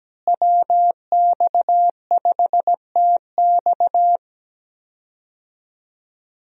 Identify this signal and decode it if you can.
Morse code, heard as a keyed tone, about 17 wpm, 700 Hz. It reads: WX5TX
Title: USA Call Signs - 5 Characters - Repeated 3x - 17wpm